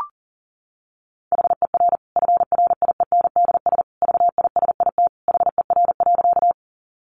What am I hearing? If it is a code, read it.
Morse code, heard as a keyed tone, about 37 wpm, 700 Hz. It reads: HER FRIENDS VISIT HER.
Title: Easy Phrases 40wpm